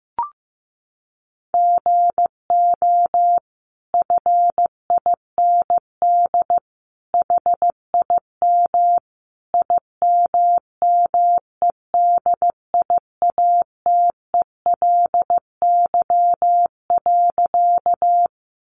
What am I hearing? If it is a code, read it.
Morse code, heard as a keyed tone, about 15 wpm, 700 Hz. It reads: GO FIND HIM IMMEDIATELY.
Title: Sentences from Top 1100 Words 15wpm